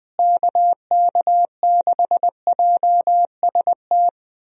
Morse code, heard as a keyed tone, about 20 wpm, 700 Hz. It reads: KK6JST
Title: USA Call Signs - 6 Characters - Repeated 2x - 20wpm